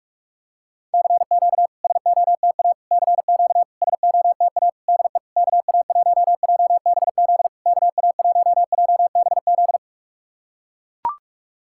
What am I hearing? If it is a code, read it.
Morse code, heard as a keyed tone, about 44 wpm, 690 Hz. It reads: CQ SOTA CQ SOTA DE KA1JBZ KA1JBZ
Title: SOTA QSO Practice 45wpm